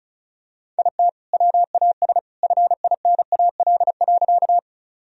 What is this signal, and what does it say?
Morse code, heard as a keyed tone, about 34 wpm, 700 Hz. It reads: IT WAS FINAL.
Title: Sentences from Top 1000 Words - Review Part 2 - 35wpm